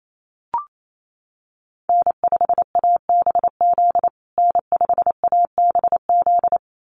Morse code, heard as a keyed tone, about 28 wpm, 700 Hz. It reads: N5ABZ N5ABZ
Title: USA Call Signs Repeated 2x 28wpm